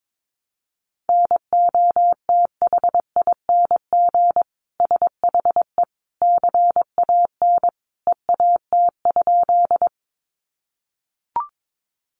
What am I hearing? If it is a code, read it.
Morse code, heard as a keyed tone, about 22 wpm, 700 Hz. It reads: NOTHING SHE CAN EAT?
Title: Sentences from Top 500 Words 22wpm